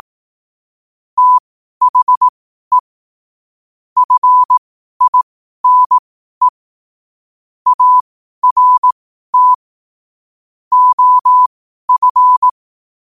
Morse code, heard as a keyed tone, about 18 wpm, spaced as for 12 wpm, 1000 Hz. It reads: THE FINE ART OF
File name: qotdfiles / 12wpm